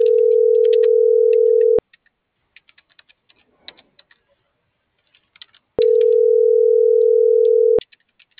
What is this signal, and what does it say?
Ringback tone (North America), 440 + 480 Hz.